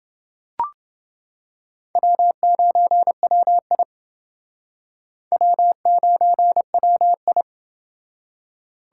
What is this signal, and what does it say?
Morse code, heard as a keyed tone, about 28 wpm, 700 Hz. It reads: W9WI W9WI